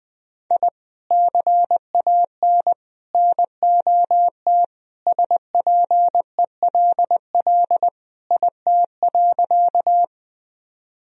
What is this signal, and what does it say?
Morse code, heard as a keyed tone, about 20 wpm, 700 Hz. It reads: I CAN NOT SPELL IT.
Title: Sentences from Top 1000 Words - Review Part 1 - 20wpm